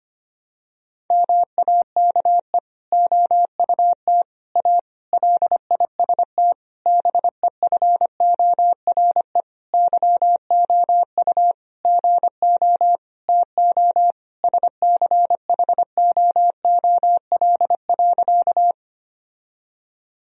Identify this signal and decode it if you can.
Morse code, heard as a keyed tone, about 25 wpm, 700 Hz. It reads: MAKE OUT A LIST BEFORE YOU GO TO SCHOOL.
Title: Sentences from Top 1000 Words - Review Part 2 - 25wpm